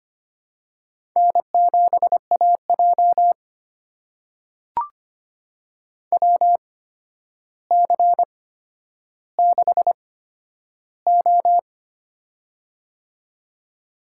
Morse code, heard as a keyed tone, about 25 wpm, spaced as for 11 wpm, 700 Hz. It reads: N7AJ WC6O
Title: USA Call Signs - 4 Characters - 8x Character Spacing - 25wpm